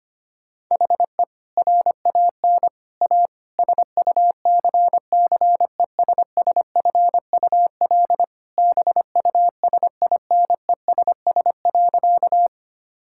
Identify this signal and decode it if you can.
Morse code, heard as a keyed tone, about 25 wpm, 700 Hz. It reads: HE RAN A SUCCESSFUL BUSINESS.
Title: Sentences from Top 1000 Words - Review Part 6 - 25wpm